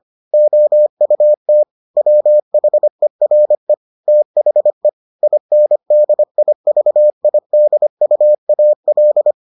Morse code, heard as a keyed tone, about 25 wpm, 600 Hz. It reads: OUT WHERE THE INDIVIDUAL